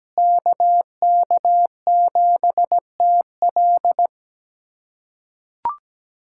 Morse code, heard as a keyed tone, about 17 wpm, 700 Hz. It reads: KK7TL
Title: USA Callsign DE Callsign - 5 Characters - 17wpm